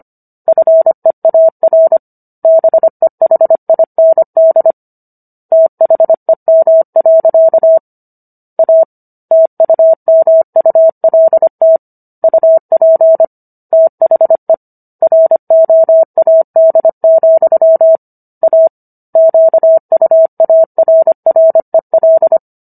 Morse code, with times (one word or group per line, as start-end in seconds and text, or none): 0.5-2.0 FEAR
2.4-4.7 BEHIND
5.5-7.8 THEM.
8.6-8.8 A
9.3-11.8 TUMULT
12.2-13.2 UP
13.7-14.5 THE
15.0-18.0 ROAD,
18.4-18.7 A
19.2-22.4 QUARREL